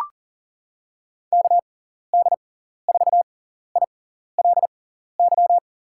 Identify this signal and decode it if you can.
Morse code, heard as a keyed tone, about 38 wpm, spaced as for 14 wpm, 700 Hz. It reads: KD4ILY
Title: USA Call Signs - 6 Characters - 6x Character Spacing - 40wpm